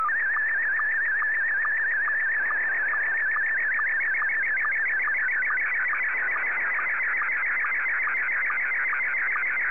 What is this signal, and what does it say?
SSTV (slow-scan television), Scottie 1.
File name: sstv / S1-VK2ELO